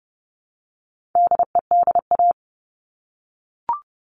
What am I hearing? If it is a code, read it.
Morse code, heard as a keyed tone, about 30 wpm, 700 Hz. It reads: DEDA
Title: All Cut Numbers - 4-Digit 30wpm